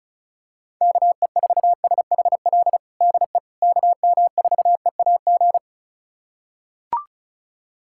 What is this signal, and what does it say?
Morse code, heard as a keyed tone, about 36 wpm, 690 Hz. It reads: KE4SHL DE KM4EAG
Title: USA Callsign DE Callsign - 6 Characters - 35wpm